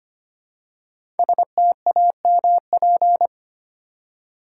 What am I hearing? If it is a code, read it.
Morse code, heard as a keyed tone, about 25 wpm, 700 Hz. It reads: STAMP
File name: 5 Letter Words - First 3 Letters Given - 25wpm